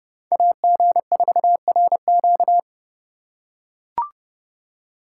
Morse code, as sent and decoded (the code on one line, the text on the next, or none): .- --. ....- .-. --.-
AG4RQ